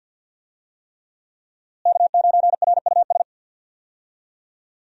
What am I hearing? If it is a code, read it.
Morse code, heard as a keyed tone, about 52 wpm, 700 Hz. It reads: K9RUS
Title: USA Call Signs - 5 Characters - Repeated 3x - 50wpm